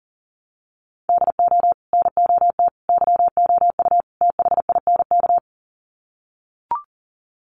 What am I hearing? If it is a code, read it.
Morse code, heard as a keyed tone, about 40 wpm, 700 Hz. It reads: DO NOT YOU THINK
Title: Sets of 4 Words 40wpm